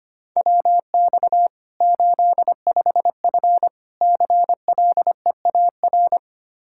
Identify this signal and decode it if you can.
Morse code, heard as a keyed tone, about 25 wpm, 700 Hz. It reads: WX 85F CLEAR
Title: Practice QSO - Exchange 2 - Info - 25wpm